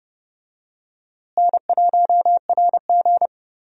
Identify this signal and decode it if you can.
Morse code, heard as a keyed tone, about 30 wpm, 700 Hz. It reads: N1RG
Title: USA Call Signs - 4 Characters - Repeated 2x - 30wpm